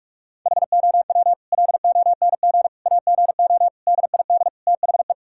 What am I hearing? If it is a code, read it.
Morse code, heard as a keyed tone, about 46 wpm, 700 Hz. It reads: HOW LONG AGO DID THE
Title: Sentences from Top 1000 Words - Review Part 6 - 45wpm